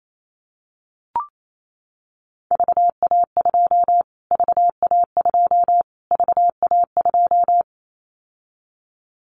Morse code, heard as a keyed tone, about 28 wpm, 700 Hz. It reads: VA2 VA2 VA2
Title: Callsign Prefixes for Top 25 DXCC - Repeated 3x - 28wpm